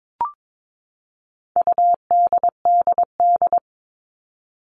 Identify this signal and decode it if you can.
Morse code, heard as a keyed tone, about 22 wpm, 700 Hz. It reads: UDDD
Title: All Cut Numbers - 4-Digit 22wpm